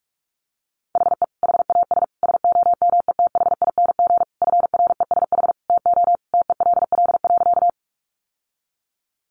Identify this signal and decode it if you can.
Morse code, heard as a keyed tone, about 44 wpm, 700 Hz. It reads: HE HAS SOMETHING FRESH TO TELL.
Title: Sentences from Top 1200 Words 45wpm